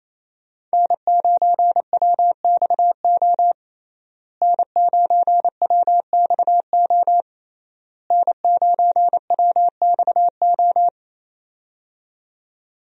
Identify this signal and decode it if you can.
Morse code, heard as a keyed tone, about 28 wpm, 700 Hz. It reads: N9WXO N9WXO N9WXO
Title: USA Call Signs - 5 Characters - Repeated 3x - 28wpm